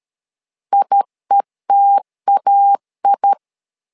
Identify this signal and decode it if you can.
Morse code, heard as a keyed tone, about 13 wpm, 800 Hz. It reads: IETAI
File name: LEZIONI / 2 UHDN / 2 U 12wpm 1338